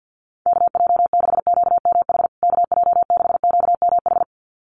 Morse code, heard as a keyed tone, about 50 wpm, 700 Hz. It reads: KJ6QMH KJ6QMH
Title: USA Call Signs Repeated 2x 50wpm